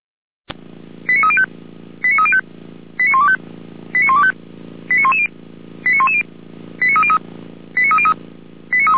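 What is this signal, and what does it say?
Five-tone selective call (ZVEI), heard as quick runs of five tones, tones of about 70 ms. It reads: 89305 89305 89125 89125 89110 89110 89303 89303
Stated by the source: Five-tone ZVEI selective call, received by SDR.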